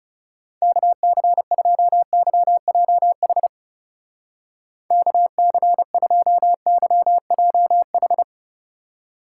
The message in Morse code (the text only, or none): KC2YJH KC2YJH